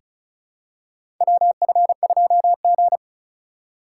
Morse code, heard as a keyed tone, about 34 wpm, 700 Hz. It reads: WF2G
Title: USA Call Signs - 4 Characters - 2x Character Spacing - 35wpm